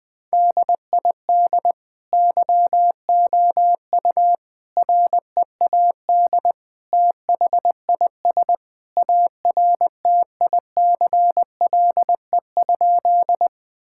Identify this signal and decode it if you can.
Morse code, heard as a keyed tone, about 21 wpm, 700 Hz. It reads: DID YOU READ THIS ARTICLE?
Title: Sentences from Top 1000 Words - Review Part 5 - 20wpm